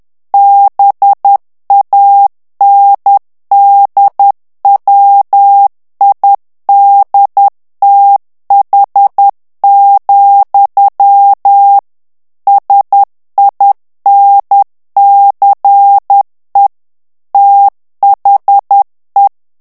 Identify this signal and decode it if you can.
Morse code, heard as a keyed tone, about 10 wpm, 800 Hz. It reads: BANDWIDTH, SINCE THE
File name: ANGLAIS texte clair 3 vitesse 10 mots minute